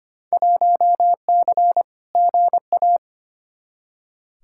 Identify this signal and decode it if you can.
Morse code, heard as a keyed tone, about 25 wpm, 700 Hz. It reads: 1C GA
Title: Field Day Exchanges - 25wpm